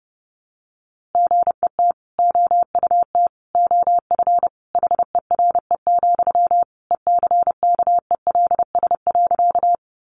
Morse code, heard as a keyed tone, about 30 wpm, 700 Hz. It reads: GET OUT OF HERE, ECKELS.
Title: A Sound Of Thunder 30wpm